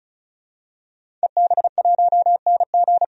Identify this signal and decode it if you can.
Morse code, heard as a keyed tone, about 36 wpm, 700 Hz. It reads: EB1NG
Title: DXCC-100 Call Signs - 5 Characters - 35wpm